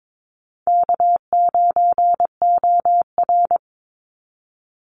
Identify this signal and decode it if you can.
Morse code, heard as a keyed tone, about 22 wpm, 700 Hz. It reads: K9OR
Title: Easy Call Signs Repeated 3x - 22wpm